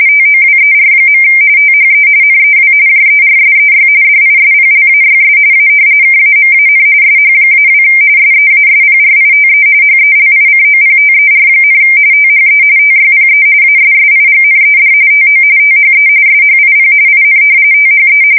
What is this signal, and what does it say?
RTTY, about 45.45 baud. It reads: TOWER. THE NEW JERSEY DISPLAYS A KAMAN SH-2 SEASPRITE HELICOPTER, THE TYPE ASSIGNED TO THE SHIP IN THE
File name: fsk170 / battleshipnj